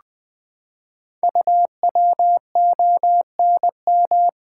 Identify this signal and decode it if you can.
Morse code, heard as a keyed tone, about 20 wpm, 700 Hz. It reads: UWONM
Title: Code Groups - 5 Letters - 20wpm